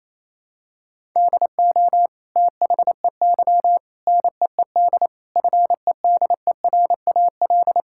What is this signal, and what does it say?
Morse code, heard as a keyed tone, about 27 wpm, 700 Hz. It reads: DO THEY NEED FEDERAL
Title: Sentences from Top 1000 Words - Review Part 5 - 28wpm